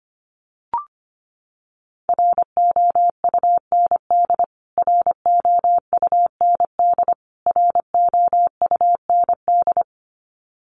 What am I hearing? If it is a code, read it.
Morse code, heard as a keyed tone, about 25 wpm, 700 Hz. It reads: ROUND ROUND ROUND
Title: Top 500 Words Repeated 25wpm